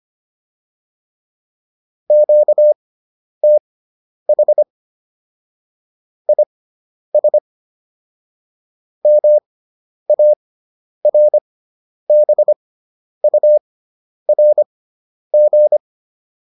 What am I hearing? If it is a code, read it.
Morse code, heard as a keyed tone, about 25 wpm, spaced as for 10 wpm, 600 Hz. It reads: QTH IS MARBURG